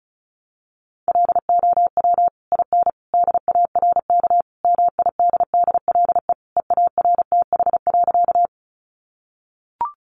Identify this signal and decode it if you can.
Morse code, heard as a keyed tone, about 34 wpm, 700 Hz. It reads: LOW IN DARK MIDDLE EARTH.
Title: Sentences from Top 900 Words 35wpm